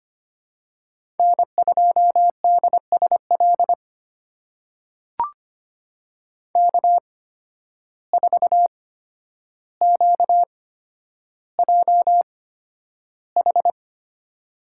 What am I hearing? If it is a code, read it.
Morse code, heard as a keyed tone, about 25 wpm, spaced as for 11 wpm, 700 Hz. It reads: N2DSL K4QJH
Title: USA Call Signs - 5 Characters - 8x Character Spacing - 25wpm